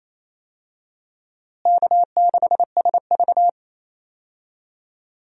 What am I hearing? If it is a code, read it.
Morse code, heard as a keyed tone, about 28 wpm, 700 Hz. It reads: K6SV
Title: USA Call Signs - 4 Characters - Repeated 2x - 28wpm